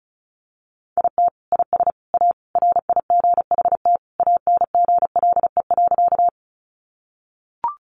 Morse code, heard as a keyed tone, about 34 wpm, 700 Hz. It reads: IT IS A RIGHT ANGLE.
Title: Sentences from Top 1500 Words 35wpm